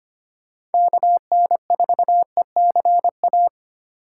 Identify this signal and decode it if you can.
Morse code, heard as a keyed tone, about 33 wpm, 700 Hz. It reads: KN4ECA